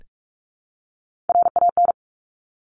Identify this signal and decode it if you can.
Morse code, heard as a keyed tone, about 46 wpm, 690 Hz. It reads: RAN
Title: Top Three Letter Combinations in Words 45wpm